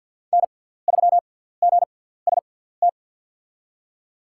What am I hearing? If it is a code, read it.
Morse code, heard as a keyed tone, about 53 wpm, spaced as for 17 wpm, 700 Hz. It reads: N3GST
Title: USA Call Signs - 5 Characters - 6x Character Spacing - 50wpm